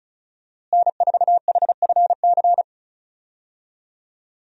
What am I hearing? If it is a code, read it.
Morse code, heard as a keyed tone, about 34 wpm, 700 Hz. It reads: N4HFC
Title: USA Call Signs - 5 Characters - Repeated 3x - 35wpm